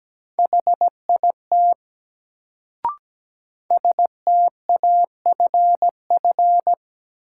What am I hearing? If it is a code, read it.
Morse code, heard as a keyed tone, about 17 wpm, 700 Hz. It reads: HIT STAFF
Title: Sentences from Top 1200 Words - New Word Drill - 17wpm